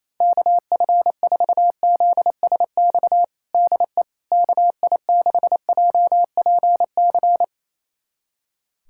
Morse code, heard as a keyed tone, about 28 wpm, 700 Hz. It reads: KF4ZSX DE KI6JPC
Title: USA Callsign DE Callsign 28wpm